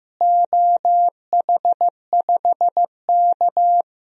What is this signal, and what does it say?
Morse code, heard as a keyed tone, about 15 wpm, 700 Hz. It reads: OH5K
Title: DXCC-100 Call Signs - 4 Characters - Repeated 2x - 15wpm